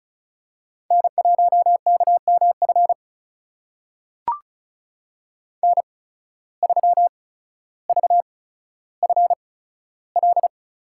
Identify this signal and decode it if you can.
Morse code, heard as a keyed tone, about 36 wpm, spaced as for 15 wpm, 700 Hz. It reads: N1KMF N3VFL
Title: USA Call Signs - 5 Characters - 8x Character Spacing - 35wpm